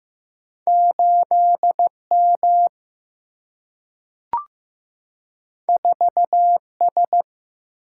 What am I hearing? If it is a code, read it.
Morse code, heard as a keyed tone, about 15 wpm, 700 Hz. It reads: 8 M 4 S
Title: Single-Letter Single-Number 15wpm